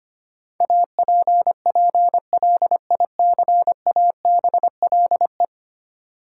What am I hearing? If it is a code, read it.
Morse code, heard as a keyed tone, about 25 wpm, 700 Hz. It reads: APPLICABLE
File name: Ten Letter Words - 25wpm